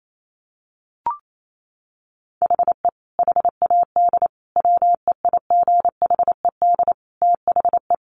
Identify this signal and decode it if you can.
Morse code, heard as a keyed tone, about 28 wpm, 700 Hz. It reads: HE HAD WEIGHED THE
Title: A Sound Of Thunder 28wpm